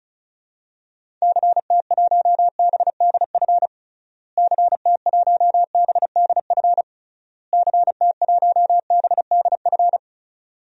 Morse code, heard as a keyed tone, about 36 wpm, 700 Hz. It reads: CT1BDF CT1BDF CT1BDF
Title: DXCC-100 Call Signs - 6 Characters - Repeated 3x - 35wpm